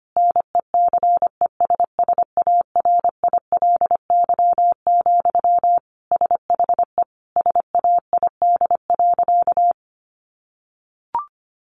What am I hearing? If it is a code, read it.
Morse code, heard as a keyed tone, about 25 wpm, 700 Hz. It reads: NECESSARILY, SHE SAID.
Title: Sentences from Top 1400 Words 25wpm